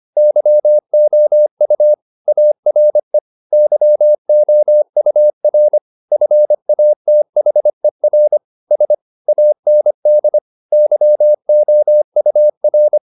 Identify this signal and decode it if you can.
Morse code, heard as a keyed tone, about 25 wpm, 600 Hz. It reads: YOU ARE YOUR FATHER S AND YOUR
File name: the old man and the sea part 1 25wpm